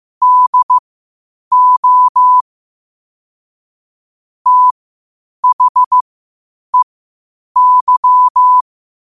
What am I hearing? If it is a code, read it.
Morse code, heard as a keyed tone, about 15 wpm, spaced as for 8 wpm, 1000 Hz. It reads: DO THEY
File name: qotdfiles / 8wpm